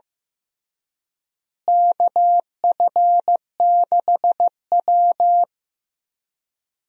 Morse code, heard as a keyed tone, about 15 wpm, 700 Hz. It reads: KF6W
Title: USA Call Signs - 4 Characters - Repeated 2x - 15wpm